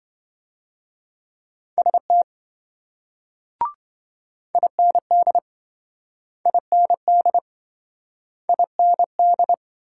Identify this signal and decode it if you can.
Morse code, heard as a keyed tone, about 28 wpm, 700 Hz. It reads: ST IND IND IND